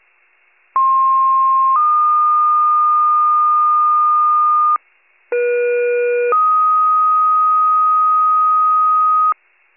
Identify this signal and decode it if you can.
Two-tone page, heard as a one-second tone, then a three-second tone, about 1060 then 1220 Hz, about 480 then 1250 Hz. It is heard from 0.7 to 9.3 s.